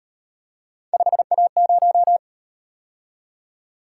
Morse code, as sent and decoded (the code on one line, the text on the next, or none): ..... .- -----
5A0